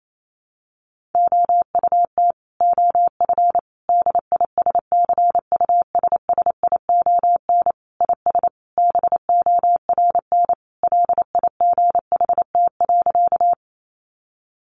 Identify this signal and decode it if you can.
Morse code, heard as a keyed tone, about 28 wpm, 700 Hz. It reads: OUT OF DISCUSSION IS BORN LIGHT.
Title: Sentences from Top 1100 Words 28wpm